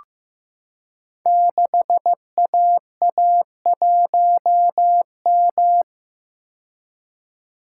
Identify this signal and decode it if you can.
Morse code, heard as a keyed tone, about 15 wpm, 700 Hz. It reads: 6AA1M
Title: Code Groups - 5 Letter:Numbers - 15wpm